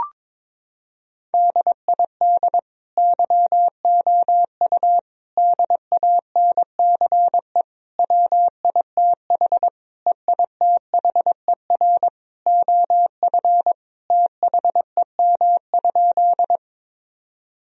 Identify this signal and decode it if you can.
Morse code, heard as a keyed tone, about 21 wpm, 700 Hz. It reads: DID YOU DANCE WITH EITHER OF THEM?
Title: Sentences from Top 1000 Words - Review Part 4 - 22wpm